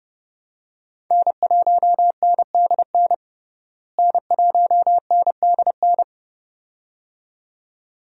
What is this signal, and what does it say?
Morse code, heard as a keyed tone, about 30 wpm, 700 Hz. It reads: N1NDN N1NDN
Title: USA Call Signs - 5 Characters - Repeated 2x - 30wpm